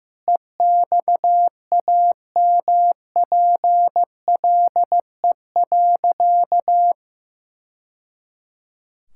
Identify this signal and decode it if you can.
Morse code, heard as a keyed tone, about 15 wpm, 700 Hz. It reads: EXAMPLE.
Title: Sentences from Top 1000 Words - Review Part 5 - 15wpm